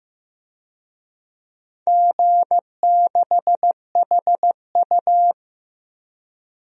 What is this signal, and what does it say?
Morse code, heard as a keyed tone, about 15 wpm, 700 Hz. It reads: G6HU
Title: DXCC-100 Call Signs - 4 Characters - Repeated 3x - 15wpm